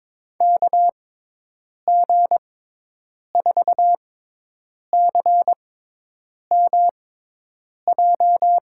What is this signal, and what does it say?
Morse code, heard as a keyed tone, about 22 wpm, spaced as for 8 wpm, 700 Hz. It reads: KG4CMJ
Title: USA Call Signs - 6 Characters - 6x Character Spacing - 22wpm